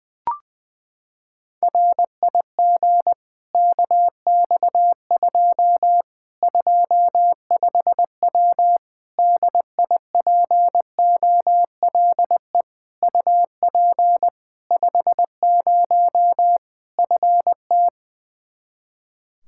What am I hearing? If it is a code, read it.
Morse code, heard as a keyed tone, about 21 wpm, 700 Hz. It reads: RIG KX2 25W DIPOLE UP 50 FT
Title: Practice QSO - Exchange 2 - Info - 20wpm